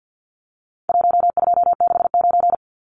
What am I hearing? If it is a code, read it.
Morse code, heard as a keyed tone, about 50 wpm, 700 Hz. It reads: 1269
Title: Numbers - Contest Style - 4-Digit 50wpm